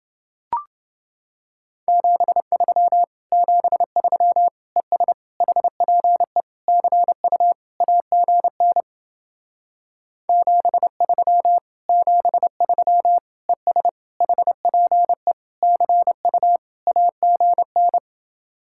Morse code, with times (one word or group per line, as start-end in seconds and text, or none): 1.9-3.0 73
3.3-4.5 73
4.8-5.1 ES
5.4-6.4 HPE
6.7-7.5 CU
7.8-8.8 AGN
10.3-11.6 73
11.9-13.2 73
13.5-13.9 ES
14.2-15.3 HPE
15.6-16.6 CU
16.9-18.0 AGN